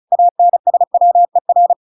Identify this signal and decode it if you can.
Morse code, heard as a keyed tone, about 34 wpm, 700 Hz. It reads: ANSWER